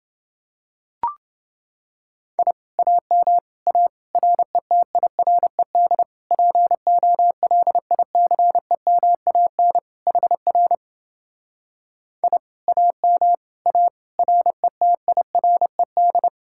Morse code, heard as a keyed tone, about 28 wpm, 700 Hz. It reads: I AM A RETIRED POLICEMAN HR I AM A RETIRED